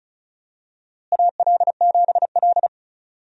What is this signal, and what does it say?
Morse code, heard as a keyed tone, about 34 wpm, 700 Hz. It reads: AL7L
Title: USA Call Signs - 4 Characters - Repeated 3x - 35wpm